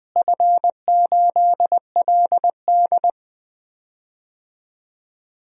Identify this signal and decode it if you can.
Morse code, heard as a keyed tone, about 20 wpm, 700 Hz. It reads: F8LD
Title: DXCC-100 Call Signs - 4 Characters - Repeated 2x - 20wpm